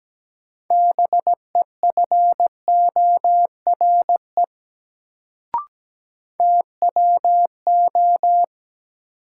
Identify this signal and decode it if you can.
Morse code, heard as a keyed tone, about 17 wpm, 700 Hz. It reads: BEFORE TWO
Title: Sets of 2 Words - New Word Drill - 17wpm